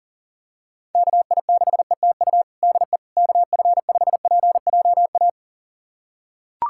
Morse code, heard as a keyed tone, about 37 wpm, 690 Hz. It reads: KI6ETU DE KF5PJA
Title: USA Callsign DE Callsign - 6 Characters - 40wpm